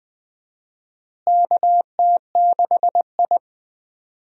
Morse code, heard as a keyed tone, about 19 wpm, 700 Hz. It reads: KT6I
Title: USA Call Signs - 4 Characters - Repeated 3x - 20wpm